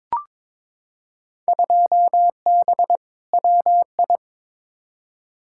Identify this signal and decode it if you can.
Morse code, heard as a keyed tone, about 22 wpm, 700 Hz. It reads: 2B WI
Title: Field Day Exchanges - 22wpm